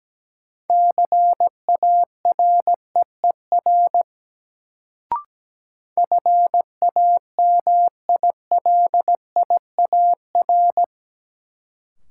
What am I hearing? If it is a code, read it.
Morse code, heard as a keyed tone, about 17 wpm, 700 Hz. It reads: CAREER FAMILIAR
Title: Sentences from Top 1000 Words - New Word Drill - 17wpm